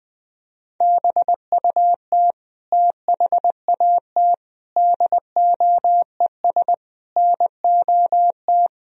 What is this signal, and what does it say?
Morse code, heard as a keyed tone, about 20 wpm, 700 Hz. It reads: BUT THAT DOES NOT